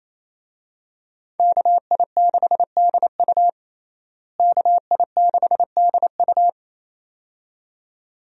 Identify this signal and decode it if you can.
Morse code, heard as a keyed tone, about 27 wpm, 700 Hz. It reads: KI6DU KI6DU
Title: USA Call Signs - 5 Characters - Repeated 2x - 28wpm